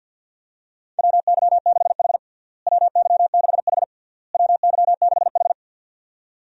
Morse code, heard as a keyed tone, about 52 wpm, 700 Hz. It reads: WY6H WY6H WY6H
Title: USA Call Signs - 4 Characters - Repeated 3x - 50wpm